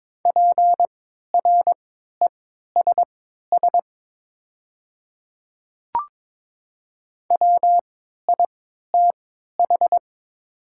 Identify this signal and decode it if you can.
Morse code, heard as a keyed tone, about 22 wpm, spaced as for 12 wpm, 700 Hz. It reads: PRESS WITH
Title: 3x Character Spacing - Top 500 Words - 22wpm